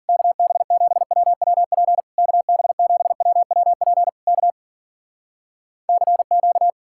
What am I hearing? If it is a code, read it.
Morse code, heard as a keyed tone, about 60 wpm, spaced as for 53 wpm, 700 Hz. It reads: KB7WWP KB7WWP K CQ